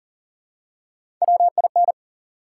Morse code, heard as a keyed tone, about 38 wpm, 700 Hz. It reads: WIN
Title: Three Letter Words - 40wpm